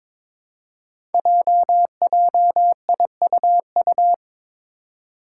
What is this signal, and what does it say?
Morse code, heard as a keyed tone, about 22 wpm, 700 Hz. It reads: JJIUU
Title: Five Challenging Letter-Number Combinations 22wpm